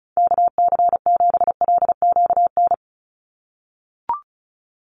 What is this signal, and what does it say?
Morse code, heard as a keyed tone, about 36 wpm, 700 Hz. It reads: KC7LQN
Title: USA Call Signs 35wpm